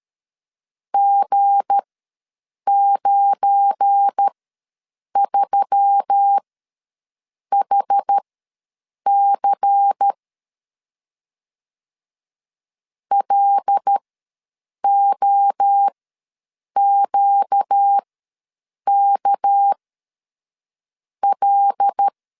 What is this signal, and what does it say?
Morse code, heard as a keyed tone, about 13 wpm, spaced as for 7 wpm, 800 Hz. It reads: G93HC LOQKL
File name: LEZIONI / 8 YJ45 / 8 Y ws 12wpm 3409